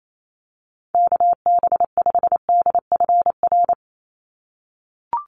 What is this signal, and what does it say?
Morse code, heard as a keyed tone, about 28 wpm, 700 Hz. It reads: KB5DFR
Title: USA Call Signs 28wpm